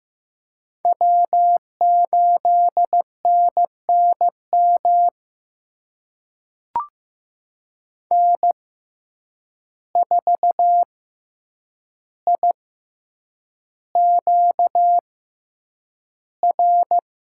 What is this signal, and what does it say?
Morse code, heard as a keyed tone, about 15 wpm, spaced as for 8 wpm, 700 Hz. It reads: W8NNM N4IQR
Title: USA Call Signs - 5 Characters - 6x Character Spacing - 15wpm